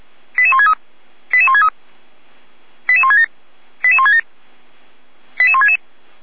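Five-tone selective call (ZVEI), heard as quick runs of five tones, tones of about 70 ms. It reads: 88163 88163 88167 88167 88166
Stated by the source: Received by SDR; ZVEI five-tone code 88063.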